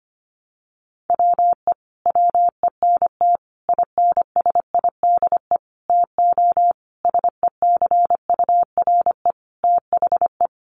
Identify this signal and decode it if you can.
Morse code, heard as a keyed tone, about 25 wpm, 700 Hz. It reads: WE WENT INSIDE TO SECURE THE
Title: Sentences from Top 1200 Words 25wpm